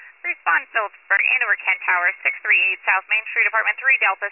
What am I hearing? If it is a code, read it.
Emergency Alert System (SAME) tones, heard as harsh buzzing bursts, with speech. It is heard from 0.2 to 3.8 s.